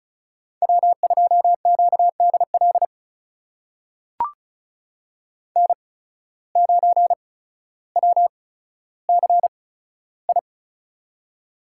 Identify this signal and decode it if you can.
Morse code, heard as a keyed tone, about 35 wpm, spaced as for 15 wpm, 700 Hz. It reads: W2QDL N9WCI
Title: USA Call Signs - 5 Characters - 8x Character Spacing - 35wpm